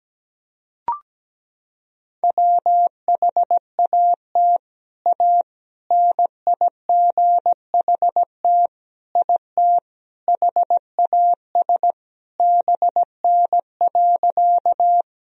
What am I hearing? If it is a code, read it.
Morse code, heard as a keyed tone, about 17 wpm, 700 Hz. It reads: WHAT A NIGHT IT HAS BN.
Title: Sentences from Top 300 Words and Abbreviations 17wpm